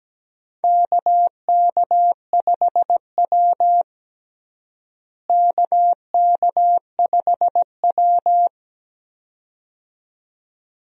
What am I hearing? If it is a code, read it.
Morse code, heard as a keyed tone, about 17 wpm, 700 Hz. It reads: KK5W KK5W
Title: USA Call Signs - 4 Characters - Repeated 2x - 17wpm